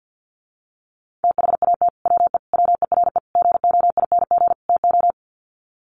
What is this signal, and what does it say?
Morse code, heard as a keyed tone, about 50 wpm, 700 Hz. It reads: THAT WE WERE GOING TO
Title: Sets of 5 Words Encore2 50wpm